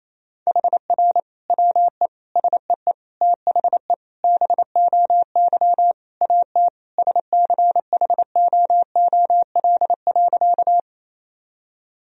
Morse code, heard as a keyed tone, about 27 wpm, 700 Hz. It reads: HR WE SEE THE BOY AT SCHOOL.
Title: Sentences from Top 300 Words and Abbreviations 28wpm